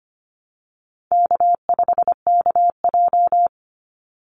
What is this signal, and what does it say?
Morse code, heard as a keyed tone, about 25 wpm, 700 Hz. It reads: K5KJ
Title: Easy Call Signs - 25wpm